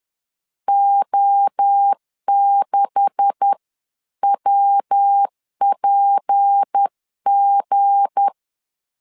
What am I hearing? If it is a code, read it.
Morse code, heard as a keyed tone, about 10 wpm, 800 Hz. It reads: O6WPG
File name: CW / DET / 1 9 1500C S A 10wpm 4511